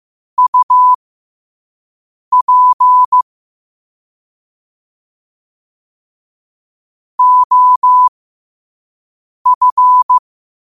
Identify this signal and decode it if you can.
Morse code, heard as a keyed tone, about 15 wpm, spaced as for 5 wpm, 1000 Hz. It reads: UP OF
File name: qotdfiles / 5wpm